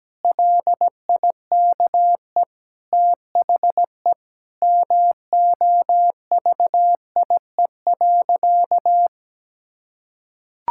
Morse code, heard as a keyed tone, about 17 wpm, 700 Hz. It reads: LIKE THE MOVIE.
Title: Sentences from Top 1000 Words - Review Part 5 - 17wpm